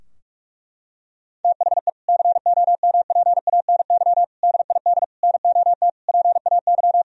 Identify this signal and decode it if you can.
Morse code, heard as a keyed tone, about 41 wpm, 700 Hz. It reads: THE COMPANY DID NOT PAY
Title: Sentences from Top 1000 Words - Review Part 3 - 45wpm